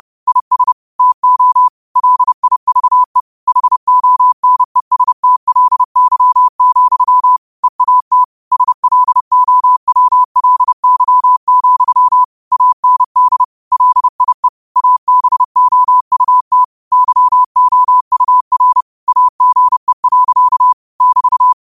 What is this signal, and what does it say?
Morse code, heard as a keyed tone, about 30 wpm, 1000 Hz. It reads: IS TO LIVE HONESTLY, EAT SLOWLY, AND LIE ABOUT YOUR AGE. <BT>
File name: qotdfiles / 30wpm